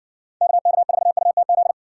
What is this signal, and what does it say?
Morse code, heard as a keyed tone, about 60 wpm, 700 Hz. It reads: KK4UTZ